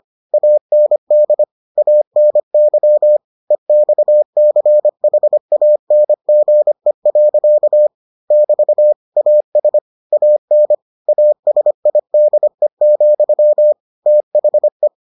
Morse code, heard as a keyed tone, about 25 wpm, 600 Hz. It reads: AND ANY EXCHANGE. <BT> AS AN ASIDE, THE